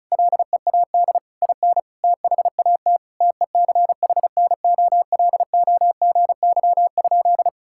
Morse code, heard as a keyed tone, about 36 wpm, 700 Hz. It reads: LEAD IN THAT TECHNOLOGY?
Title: Sentences from Top 1000 Words - Review Part 5 - 35wpm